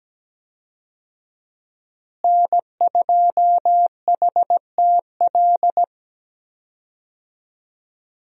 Morse code, heard as a keyed tone, about 17 wpm, 700 Hz. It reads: N2HTL
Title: USA Call Signs - 5 Characters - Repeated 3x - 17wpm